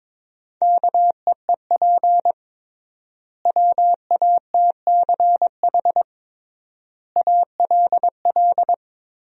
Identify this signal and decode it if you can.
Morse code, heard as a keyed tone, about 22 wpm, 700 Hz. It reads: KEEP WATCH ALL
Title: 3x Word-Spacing - Sentences from Top 300 Words - 22wpm